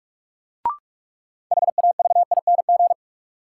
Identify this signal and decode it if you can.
Morse code, heard as a keyed tone, about 46 wpm, 700 Hz. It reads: HAVING
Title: Sets of 5 Words Encore - New Word Drill - 45wpm